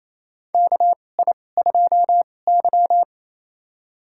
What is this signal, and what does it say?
Morse code, heard as a keyed tone, about 27 wpm, spaced as for 20 wpm, 690 Hz. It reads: KI2Y
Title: USA Call Signs - 4 Characters - 2x Character Spacing - 28wpm